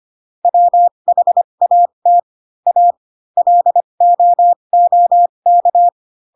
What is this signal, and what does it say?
Morse code, heard as a keyed tone, about 25 wpm, spaced as for 22 wpm, 700 Hz. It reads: WHAT A LOOK